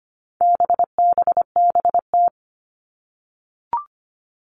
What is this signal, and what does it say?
Morse code, heard as a keyed tone, about 25 wpm, 700 Hz. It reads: BBBT
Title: All Cut Numbers - 4-Digit 25wpm